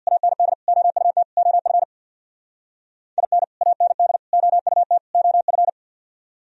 Morse code, heard as a keyed tone, about 62 wpm, 700 Hz. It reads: AND OUT OF IN AND OUT OF
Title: Speed-Racing - Sets of 4 Words 50wpm